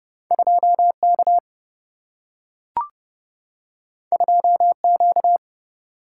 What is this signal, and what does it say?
Morse code, heard as a keyed tone, about 30 wpm, 700 Hz. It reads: 2K 2Q
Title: Single-Letter Single-Number 30wpm